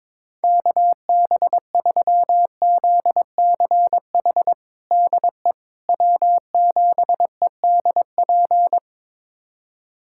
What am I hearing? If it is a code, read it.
Morse code, heard as a keyed tone, about 22 wpm, 700 Hz. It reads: KB3ZCH DE W7EDP
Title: USA Callsign DE Callsign 22wpm